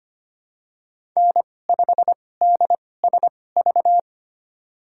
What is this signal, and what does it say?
Morse code, heard as a keyed tone, about 25 wpm, spaced as for 18 wpm, 700 Hz. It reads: N5DSV
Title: USA Call Signs - 2x Character Spacing - 25wpm